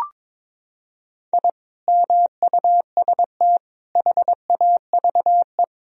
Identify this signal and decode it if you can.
Morse code, heard as a keyed tone, about 21 wpm, 700 Hz. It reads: I MUST HAVE
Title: Sentences from Top 1000 Words - Review Part 2 - 22wpm